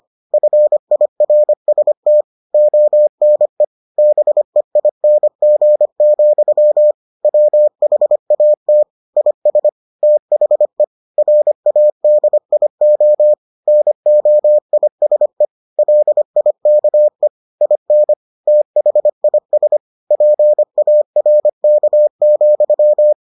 Morse code, heard as a keyed tone, about 25 wpm, 600 Hz. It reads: FIRST ONE BEING, WHAT IS THE RADIO NOISE LIKE IN THIS PARK,